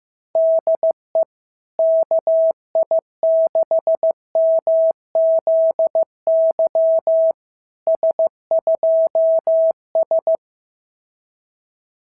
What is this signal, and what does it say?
Morse code, heard as a keyed tone, about 15 wpm, 650 Hz. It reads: DE KI6MZY S2S
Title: SOTA QSO Practice 15wpm